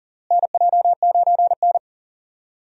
Morse code, heard as a keyed tone, about 40 wpm, 700 Hz. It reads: NJ9N
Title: USA Call Signs - 4 Characters - Repeated 2x - 40wpm